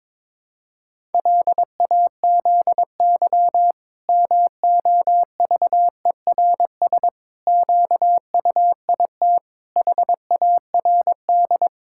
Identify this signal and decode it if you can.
Morse code, heard as a keyed tone, about 21 wpm, 700 Hz. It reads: LAZY MOVERS QUIT HARD
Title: Pangram Challenge v2 - 22wpm